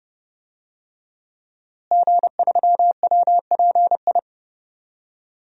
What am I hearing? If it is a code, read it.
Morse code, heard as a keyed tone, about 30 wpm, 700 Hz. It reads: G3WPI
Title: DXCC-100 Call Signs - 5 Characters - Repeated 2x - 30wpm